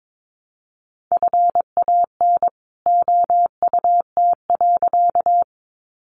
Morse code, heard as a keyed tone, about 22 wpm, 700 Hz. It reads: FAN OUT.
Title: Sentences from Top 1300 Words 22wpm